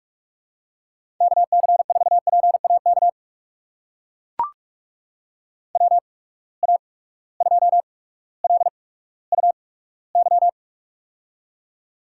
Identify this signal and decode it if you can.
Morse code, heard as a keyed tone, about 42 wpm, spaced as for 19 wpm, 700 Hz. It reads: KC4PAK WA2LUY
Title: USA Call Signs - 6 Characters - 8x Character Spacing - 45wpm